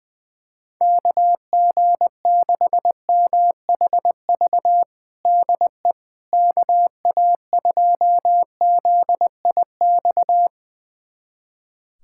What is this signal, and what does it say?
Morse code, heard as a keyed tone, about 21 wpm, 700 Hz. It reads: KG6MHV DE KA2ZIX
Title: USA Callsign DE Callsign 20wpm